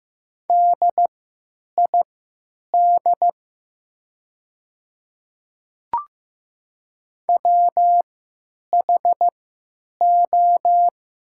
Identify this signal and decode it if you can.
Morse code, heard as a keyed tone, about 15 wpm, spaced as for 9 wpm, 700 Hz. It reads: DID WHO
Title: 3x Character Spacing - Top 100 Words - 15wpm